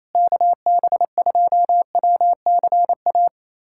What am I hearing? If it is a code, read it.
Morse code, heard as a keyed tone, about 28 wpm, 700 Hz. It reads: KB2WCA
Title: USA Callsign DE Callsign 28wpm